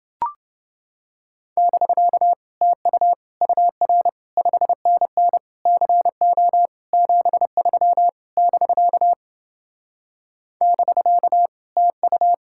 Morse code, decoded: <BK> TU UR 5NN CO 73 <BK> <BK> TU

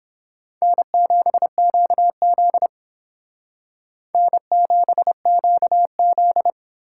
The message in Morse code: N7QZ N7QZ

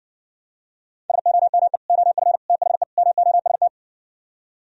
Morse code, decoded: SOME OF THE MOST

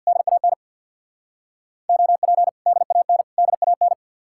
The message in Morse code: DAN OP DAN DAN

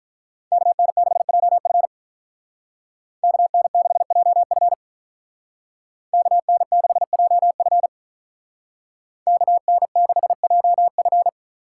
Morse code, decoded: KN6JF KN6JF KN6JF KN6JF

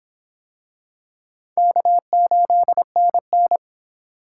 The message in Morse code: K8NN

